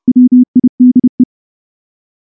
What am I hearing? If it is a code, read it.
Morse code, heard as a keyed tone, about 28 wpm, 250 Hz. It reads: WIDE
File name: morse-training-30WPM